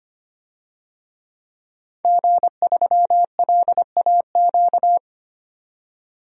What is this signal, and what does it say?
Morse code, heard as a keyed tone, about 25 wpm, 700 Hz. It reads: G3LAQ
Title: DXCC-100 Call Signs - 5 Characters - 25wpm